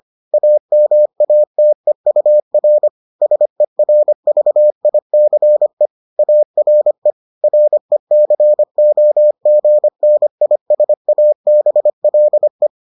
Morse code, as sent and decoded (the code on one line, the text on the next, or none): .- -- .- - . ..- .-. / ... .-. ...- .. -.-. / .- .-. / .-. -.-. --- --. -. .. ... .- -... .-.. .
AMATEUR SERVICE ARE RECOGNISABLE